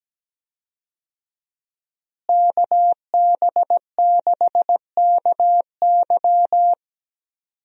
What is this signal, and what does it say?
Morse code, heard as a keyed tone, about 17 wpm, 700 Hz. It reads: KB6KY
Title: USA Call Signs - 5 Characters - Repeated 3x - 17wpm